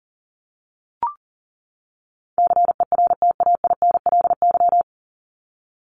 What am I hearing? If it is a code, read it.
Morse code, heard as a keyed tone, about 40 wpm, 700 Hz. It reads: CERTAINLY
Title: Top 751-1000 Words 40wpm